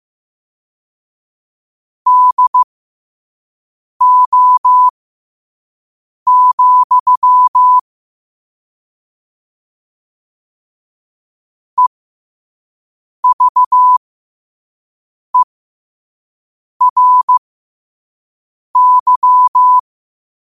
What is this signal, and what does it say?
Morse code, heard as a keyed tone, about 15 wpm, spaced as for 5 wpm, 1000 Hz. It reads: DO, EVERY
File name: qotdfiles / 5wpm